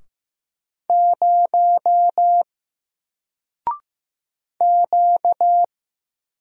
Morse code, heard as a keyed tone, about 15 wpm, 700 Hz. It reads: 0 Q
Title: Single Character (Letter:Number:Punctuation:Prosign) 15wpm